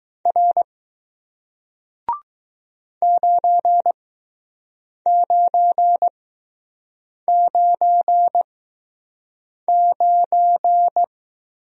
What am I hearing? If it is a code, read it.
Morse code, heard as a keyed tone, about 20 wpm, spaced as for 6 wpm, 700 Hz. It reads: R 9999